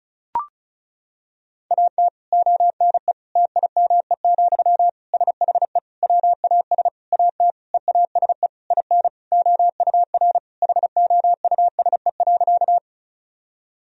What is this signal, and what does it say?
Morse code, heard as a keyed tone, about 36 wpm, 700 Hz. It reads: AT ONE TIME, SHE WAS AT EASE IN OUR HOUSE.
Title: Sentences from Top 1000 Words - Review Part 1 - 35wpm